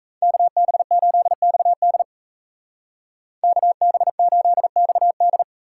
Morse code, heard as a keyed tone, about 37 wpm, 700 Hz. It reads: KB8XD KB8XD